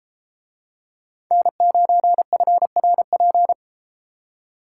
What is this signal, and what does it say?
Morse code, heard as a keyed tone, about 33 wpm, 700 Hz. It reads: N9FRP